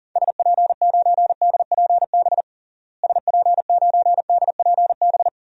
Morse code, heard as a keyed tone, about 40 wpm, 700 Hz. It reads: SP9DPB SP9DPB
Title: DXCC-100 Call Signs - 6 Characters - Repeated 3x - 40wpm